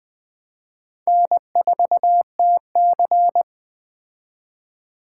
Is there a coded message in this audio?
Yes: Morse code from 1.1 to 3.4 s, heard as a keyed tone, about 20 wpm, 700 Hz. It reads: N4TC